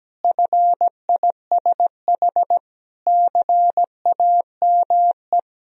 Morse code, heard as a keyed tone, about 17 wpm, 700 Hz. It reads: FISH CAME
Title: Sentences from Top 1000 Words - Review Part 2 - 17wpm